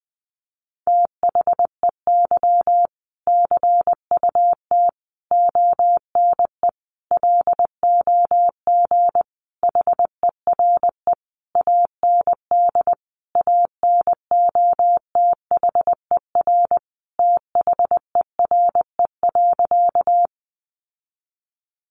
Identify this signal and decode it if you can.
Morse code, heard as a keyed tone, about 19 wpm, 700 Hz. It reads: THEY CUT ONE LOG HERE AND ANOTHER THERE.
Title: Sentences from Top 1400 Words 20wpm